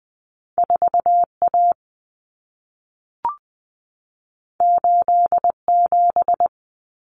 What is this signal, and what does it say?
Morse code, heard as a keyed tone, about 20 wpm, 700 Hz. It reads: 4A 87
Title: Common Cut Numbers - 2-Digit 20wpm